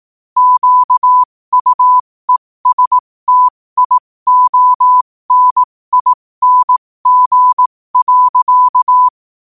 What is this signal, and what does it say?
Morse code, heard as a keyed tone, about 18 wpm, spaced as for 15 wpm, 1000 Hz. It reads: QUESTIONING.
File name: qotdfiles / 15wpm